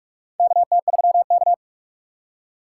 Morse code, heard as a keyed tone, about 43 wpm, 700 Hz. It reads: KT3K
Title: USA Call Signs - 4 Characters - 6x Character Spacing - 45wpm